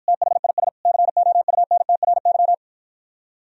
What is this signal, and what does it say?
Morse code, heard as a keyed tone, about 59 wpm, 700 Hz. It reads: THIS COUNTRY